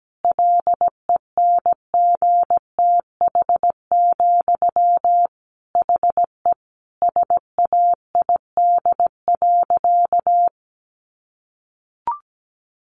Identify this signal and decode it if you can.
Morse code, heard as a keyed tone, about 17 wpm, 700 Hz. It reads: LENGTH, HE SAID.
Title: Sentences from Top 1000 Words 17wpm